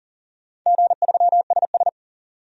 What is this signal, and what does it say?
Morse code, heard as a keyed tone, about 40 wpm, 700 Hz. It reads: G3SS
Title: DXCC-100 Call Signs - 4 Characters - 40wpm